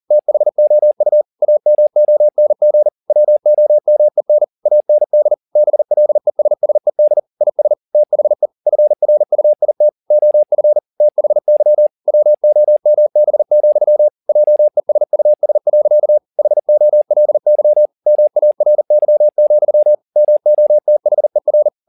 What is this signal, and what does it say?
Morse code, heard as a keyed tone, about 40 wpm, 600 Hz. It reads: THOU AMONG WOMEN AND BLESSED IS THE FRUIT OF THY WOMB, JESUS. HOLY MARY, MOTHER